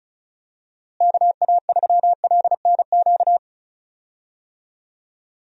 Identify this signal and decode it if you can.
Morse code, heard as a keyed tone, about 36 wpm, 700 Hz. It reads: KA3LNQ
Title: USA Call Signs - 6 Characters - Repeated 2x - 35wpm